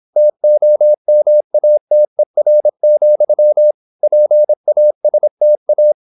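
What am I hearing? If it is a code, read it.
Morse code, heard as a keyed tone, about 26 wpm, 600 Hz. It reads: TOMATER, PASTA
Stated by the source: High-speed (QRQ) Morse at 26 wpm.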